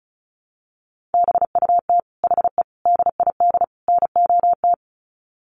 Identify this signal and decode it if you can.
Morse code, heard as a keyed tone, about 34 wpm, 700 Hz. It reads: BUT HE DID NOT
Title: Sets of 4 Words 35wpm